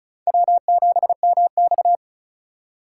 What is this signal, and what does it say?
Morse code, heard as a keyed tone, about 36 wpm, 700 Hz. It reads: W7MX
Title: USA Call Signs - 4 Characters - 4x Character Spacing - 35wpm